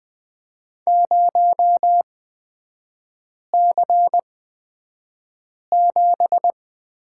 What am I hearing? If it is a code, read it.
Morse code, heard as a keyed tone, about 20 wpm, spaced as for 5 wpm, 700 Hz. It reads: 0C7
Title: 243 Lesson 39e - Letter z - Three Characters 20wpm